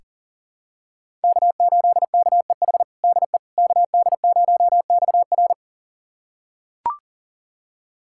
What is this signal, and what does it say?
Morse code, heard as a keyed tone, about 40 wpm, 700 Hz. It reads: K8KEH DE KD0XR
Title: USA Callsign DE Callsign - 5 Characters - 40wpm